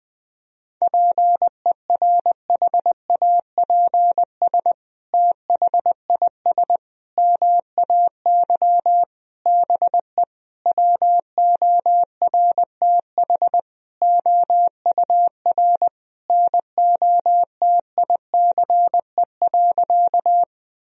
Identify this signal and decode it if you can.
Morse code, heard as a keyed tone, about 21 wpm, 700 Hz. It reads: PERHAPS THIS MAY BE WORTH OUR NOTICE.
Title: Sentences from Top 1000 Words - Review Part 2 - 20wpm